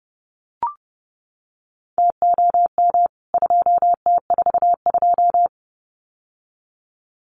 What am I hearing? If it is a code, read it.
Morse code, heard as a keyed tone, about 30 wpm, 700 Hz. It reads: TOM 2T42
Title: CWT Exchange 30wpm